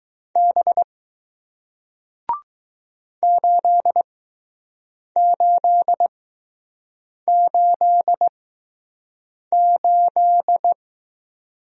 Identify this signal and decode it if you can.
Morse code, heard as a keyed tone, about 20 wpm, spaced as for 6 wpm, 700 Hz. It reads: B 8888